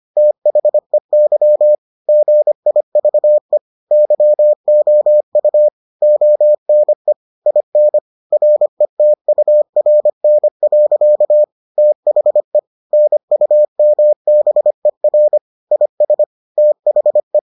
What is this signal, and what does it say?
Morse code, heard as a keyed tone, about 25 wpm, 600 Hz. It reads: THEY GIVE YOU ONE IN RETURN. THE NUMBER IS THE